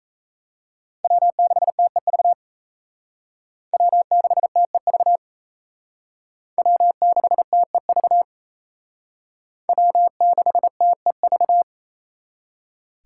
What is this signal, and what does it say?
Morse code, heard as a keyed tone, about 36 wpm, 700 Hz. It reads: W6TEV W6TEV W6TEV W6TEV